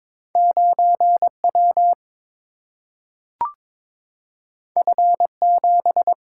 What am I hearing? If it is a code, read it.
Morse code, heard as a keyed tone, about 22 wpm, 700 Hz. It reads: 9W F7
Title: Single-Letter Single-Number 22wpm